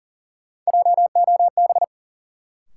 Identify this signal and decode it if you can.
Morse code, heard as a keyed tone, about 40 wpm, 700 Hz. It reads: JOB
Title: Sentences from Top 600 Words - New Word Drill - 40wpm